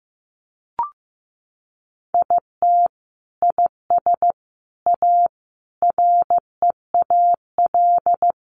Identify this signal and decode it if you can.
Morse code, heard as a keyed tone, about 15 wpm, 700 Hz. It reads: IT IS A REAL
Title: Sentences from Top 1400 Words 15wpm